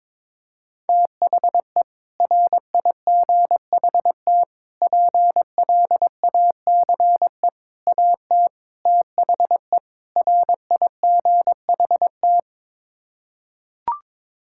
Morse code, heard as a keyed tone, about 22 wpm, 700 Hz. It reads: THE RIGHT PLACE AT THE RIGHT
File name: practice / 22wpm / Sets of 6 Words - 22wpm